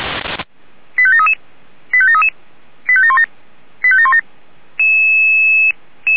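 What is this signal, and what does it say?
Five-tone selective call (ZVEI), heard as quick runs of five tones, tones of about 70 ms. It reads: 85733 85733 85717 85717
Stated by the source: ZVEI five-tone selective call 89496 received by SDR, logged by the Feuerwehr Orlamünde fire brigade.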